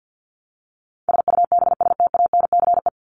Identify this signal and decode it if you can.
Morse code, heard as a keyed tone, about 50 wpm, 700 Hz. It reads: SUBSTANCE
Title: Top 1251-1500 Words 50wpm